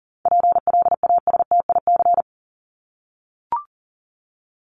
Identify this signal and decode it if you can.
Morse code, heard as a keyed tone, about 40 wpm, 700 Hz. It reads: PLASTIC
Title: Top 1001-1250 Words 40wpm